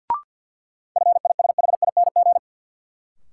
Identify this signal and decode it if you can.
Morse code, heard as a keyed tone, about 52 wpm, 700 Hz. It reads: FISHING
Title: Sentences from Top 1600 Words - New Word Drill Alt - 50wpm